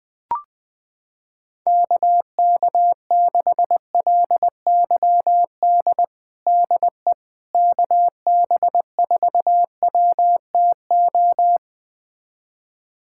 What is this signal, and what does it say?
Morse code, heard as a keyed tone, about 21 wpm, 700 Hz. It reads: KK6LYD DE KB4WTO
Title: USA Callsign DE Callsign - 6 Characters - 20wpm